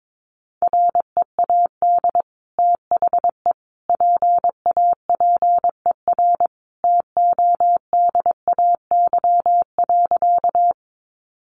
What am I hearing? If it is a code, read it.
Morse code, heard as a keyed tone, about 22 wpm, 700 Hz. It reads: READ THE PAPER TODAY.
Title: Sentences from Top 600 Words 22wpm